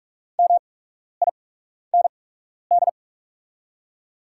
Morse code, heard as a keyed tone, about 46 wpm, spaced as for 12 wpm, 700 Hz. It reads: MIND
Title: 8x Character Spacing - Top 500 Words - 45wpm